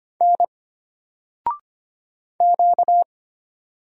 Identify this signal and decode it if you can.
Morse code, heard as a keyed tone, about 25 wpm, 700 Hz. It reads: N Q
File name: Single-Letter-Number-25wpm